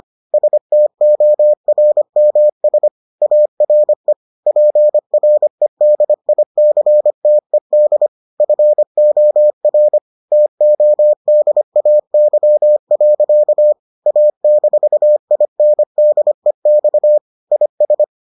Morse code, heard as a keyed tone, about 25 wpm, 600 Hz. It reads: STORMS ARE PREDICTED FOR TODAY. A-INDEX IS